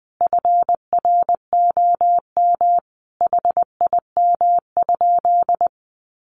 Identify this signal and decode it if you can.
Morse code, heard as a keyed tone, about 20 wpm, 700 Hz. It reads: FROM HIM?
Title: Sentences from Top 300 Words 20wpm